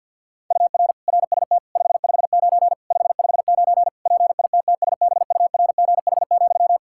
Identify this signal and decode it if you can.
Morse code, heard as a keyed tone, about 52 wpm, 700 Hz. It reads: UR RST 559 559 PITTSBURGH,
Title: Practice QSO - Exchange 1 - Beginning - 50wpm